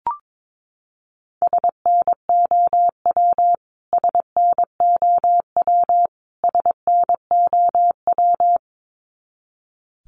Morse code, heard as a keyed tone, about 22 wpm, 700 Hz. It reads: SNOW SNOW SNOW
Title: Top 500 Words Repeated 22wpm